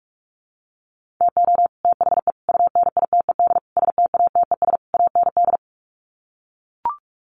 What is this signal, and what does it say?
Morse code, heard as a keyed tone, about 44 wpm, 700 Hz. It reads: TO THE UNITED STATES AND
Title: Sets of 5 Words Encore 45wpm